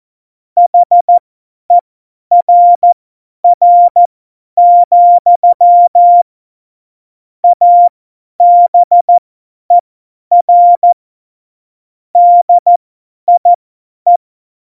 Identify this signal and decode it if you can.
Morse code, heard as a keyed tone, about 14 wpm, spaced as for 10 wpm, 700 Hz. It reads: HERR, ABER DIE
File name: Morsekurs / Witze / Witz 03. September 2019, 50 F70